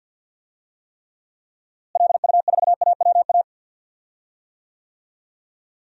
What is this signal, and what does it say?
Morse code, heard as a keyed tone, about 52 wpm, 700 Hz. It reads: LU4AWA
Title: DXCC-100 Call Signs - 6 Characters - Repeated 3x - 50wpm